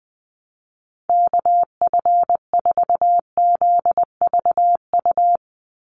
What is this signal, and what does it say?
Morse code, heard as a keyed tone, about 20 wpm, 700 Hz. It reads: KF4ZVU
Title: USA Call Signs 20wpm